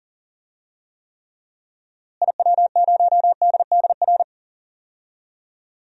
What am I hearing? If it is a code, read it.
Morse code, heard as a keyed tone, about 40 wpm, 700 Hz. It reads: IW0DDR
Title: DXCC-100 Call Signs - 6 Characters - Repeated 3x - 40wpm